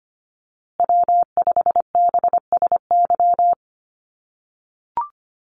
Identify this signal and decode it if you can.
Morse code, heard as a keyed tone, about 25 wpm, 700 Hz. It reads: W5BSY
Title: USA Call Signs Repeated 2x 25wpm